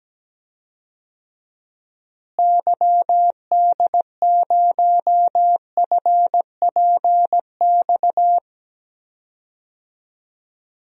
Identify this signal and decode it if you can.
Morse code, heard as a keyed tone, about 17 wpm, 700 Hz. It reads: YD0FPX
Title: DXCC-100 Call Signs - 6 Characters - Repeated 3x - 17wpm